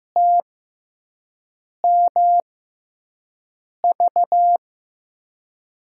Morse code, heard as a keyed tone, about 15 wpm, spaced as for 5 wpm, 700 Hz. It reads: TMV
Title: USA Call Signs - 5 Characters - 6x Character Spacing - 15wpm